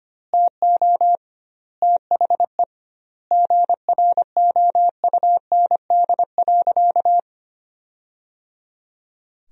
Morse code, heard as a keyed tone, about 25 wpm, 700 Hz. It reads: TO THE GROUND.
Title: 2x Word-Spacing - Sentences from Top 500 Words - 25wpm